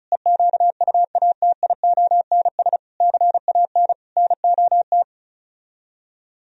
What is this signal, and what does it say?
Morse code, heard as a keyed tone, about 37 wpm, 700 Hz. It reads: EQUATIONS CAN NOT
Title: Speed-Racing - Sets of 5 Words 30wpm